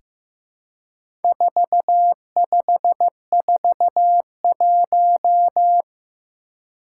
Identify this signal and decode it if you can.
Morse code, heard as a keyed tone, about 15 wpm, 700 Hz. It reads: 4541
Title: Four Challenging Letter-Number Combinations 15wpm